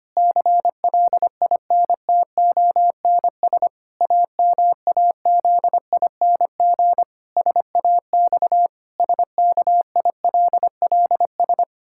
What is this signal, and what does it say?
Morse code, heard as a keyed tone, about 25 wpm, 700 Hz. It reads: CLINTONS AMAZING SAX SKILLS
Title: Pangram Challenge v2 - 25wpm